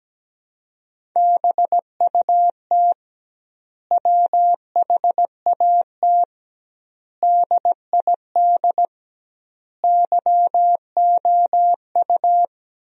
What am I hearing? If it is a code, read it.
Morse code, heard as a keyed tone, about 17 wpm, 700 Hz. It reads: BUT WHAT DID YOU
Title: 2x Word-Spacing - Sentences from Top 300 Words - 17wpm